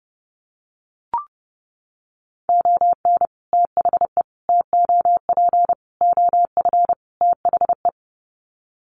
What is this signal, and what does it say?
Morse code, heard as a keyed tone, about 30 wpm, 700 Hz. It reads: ON THE TOP OF THE
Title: Sets of 5 Words Encore 30wpm